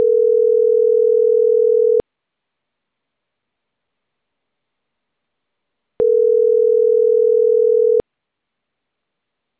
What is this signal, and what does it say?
Ringback tone (North America), 440 + 480 Hz.